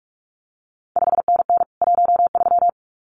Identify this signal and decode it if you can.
Morse code, heard as a keyed tone, about 46 wpm, 700 Hz. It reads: 5NN 13
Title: CQ WW DX Contest Exchange Encore - 45wpm